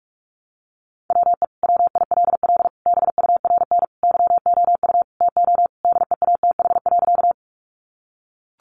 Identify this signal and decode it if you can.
Morse code, heard as a keyed tone, about 44 wpm, 700 Hz. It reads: WE WILL BURN YOU TO DEATH.
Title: Sentences from Top 1300 Words 45wpm